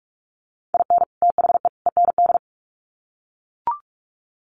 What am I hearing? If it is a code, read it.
Morse code, heard as a keyed tone, about 44 wpm, 700 Hz. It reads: IN THE END
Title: Sets of 3 Words 45wpm